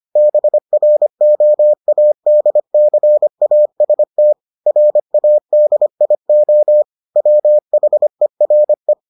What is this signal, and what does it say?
Morse code, heard as a keyed tone, about 25 wpm, 600 Hz. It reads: BROADCAST RADIO WHERE